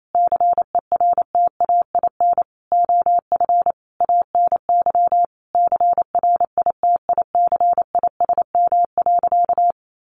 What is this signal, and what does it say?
Morse code, heard as a keyed tone, about 28 wpm, 700 Hz. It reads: CERTAIN OF ANY CRITICISM.
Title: Sentences from Top 1300 Words 28wpm